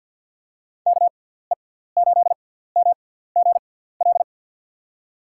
Morse code, heard as a keyed tone, about 47 wpm, spaced as for 17 wpm, 700 Hz. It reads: KE8MGL
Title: USA Call Signs - 6 Characters - 6x Character Spacing - 50wpm